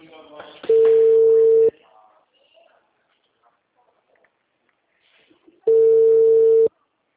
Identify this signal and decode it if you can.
Ringback tone (China, 450 Hz), with speech.
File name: out- -2003- - - .344